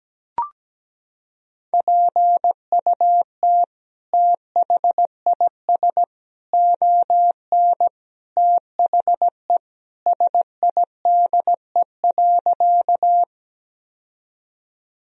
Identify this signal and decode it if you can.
Morse code, heard as a keyed tone, about 17 wpm, 700 Hz. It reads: PUT THIS ON THE SIDE.
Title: Sentences from Top 1000 Words - Review Part 1 - 17wpm